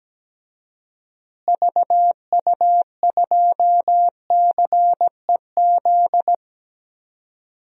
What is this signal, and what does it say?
Morse code, heard as a keyed tone, about 17 wpm, 700 Hz. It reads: VU2CEZ
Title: DXCC-100 Call Signs - 6 Characters - Repeated 2x - 17wpm